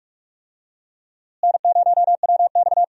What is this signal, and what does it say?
Morse code, heard as a keyed tone, about 46 wpm, 700 Hz. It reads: N0WX
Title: USA Call Signs - 4 Characters - Repeated 3x - 45wpm